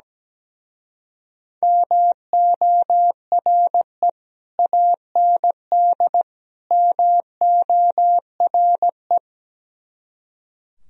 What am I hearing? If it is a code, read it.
Morse code, heard as a keyed tone, about 17 wpm, 700 Hz. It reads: MORE AND MORE
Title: Speed-Racing - Sets of 3 Words 17wpm